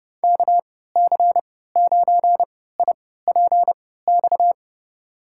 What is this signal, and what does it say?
Morse code, heard as a keyed tone, about 30 wpm, spaced as for 17 wpm, 700 Hz. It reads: KC9IPX
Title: USA Call Signs - 6 Characters - 3x Character Spacing - 30wpm